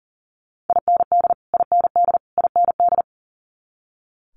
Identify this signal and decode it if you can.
Morse code, heard as a keyed tone, about 39 wpm, 700 Hz. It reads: IND IND IND
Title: Multi-letter Phonemes Repeated 3x 40wpm